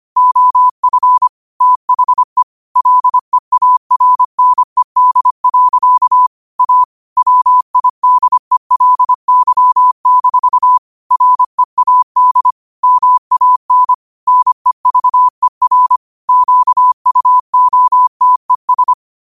Morse code, heard as a keyed tone, about 25 wpm, 1000 Hz. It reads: OF THE LEARNED. A WIDELY-READ MAN NEVER QUOTES
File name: qotdfiles / 25wpm